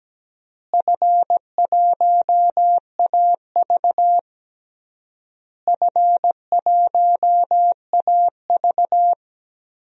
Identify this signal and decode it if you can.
Morse code, heard as a keyed tone, about 17 wpm, 700 Hz. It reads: F1AV F1AV
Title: DXCC-100 Call Signs - 4 Characters - Repeated 3x - 17wpm